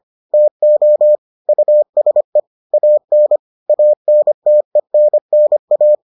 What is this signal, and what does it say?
Morse code, heard as a keyed tone, about 25 wpm, 600 Hz. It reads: TO USE AN ANTENNA